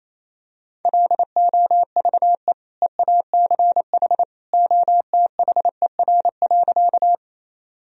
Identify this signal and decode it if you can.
Morse code, heard as a keyed tone, about 28 wpm, 700 Hz. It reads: LOVE EACH OTHER.
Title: Sentences from Top 1000 Words - Review Part 1 - 28wpm